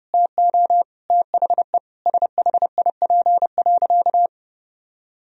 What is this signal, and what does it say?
Morse code, heard as a keyed tone, about 30 wpm, 700 Hz. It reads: TO THE SHIP.
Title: Sentences from Top 1000 Words - Review Part 2 - 30wpm